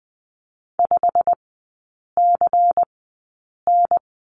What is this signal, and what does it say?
Morse code, heard as a keyed tone, about 20 wpm, spaced as for 8 wpm, 690 Hz. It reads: 5CN
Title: 093 Lesson 16e - Letter c - Three Characters 20wpm